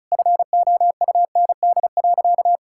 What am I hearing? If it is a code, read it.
Morse code, heard as a keyed tone, about 36 wpm, 700 Hz. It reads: FOUND.
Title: Sentences from Top 1000 Words - Review Part 5 - 35wpm